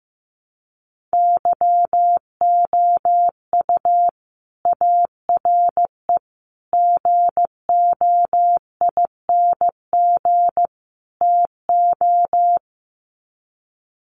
Morse code, heard as a keyed tone, about 15 wpm, 700 Hz. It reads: YOU ARE GOING TO
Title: Sets of 4 Words 15wpm